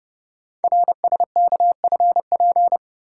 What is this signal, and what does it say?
Morse code, heard as a keyed tone, about 28 wpm, 690 Hz. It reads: RSKFP
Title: Code Groups - 5 Letters - 30wpm